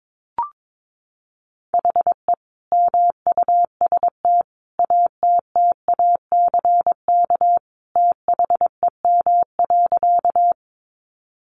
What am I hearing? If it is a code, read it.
Morse code, heard as a keyed tone, about 22 wpm, 700 Hz. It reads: HE MUST ATTACK THEM.
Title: Sentences from Top 1500 Words 22wpm